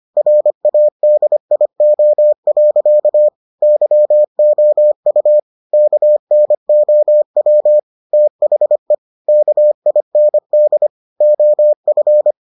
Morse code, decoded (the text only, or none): RADIO. YOU KNOW THE KIND OF